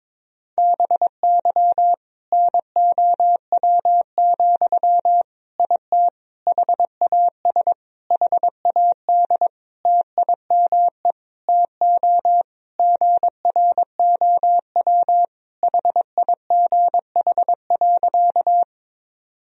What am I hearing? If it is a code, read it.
Morse code, heard as a keyed tone, about 22 wpm, 700 Hz. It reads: BY NOW, IT HAS HAD TIME TO GROW HIGH.
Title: Sentences from Top 1000 Words - Review Part 1 - 22wpm